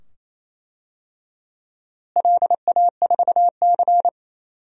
Morse code, heard as a keyed tone, about 28 wpm, 700 Hz. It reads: LA4C
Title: DXCC-100 Call Signs - 4 Characters - Repeated 2x - 28wpm